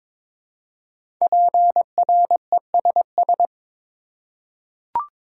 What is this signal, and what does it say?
Morse code, heard as a keyed tone, about 23 wpm, 700 Hz. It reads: PRESS
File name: 5 Letter Words - First 3 Letters Given - 22wpm